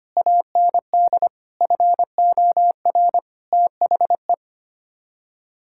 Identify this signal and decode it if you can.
Morse code, heard as a keyed tone, about 25 wpm, 700 Hz. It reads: AND FOR THE
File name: Sets of 9 Words - 25wpm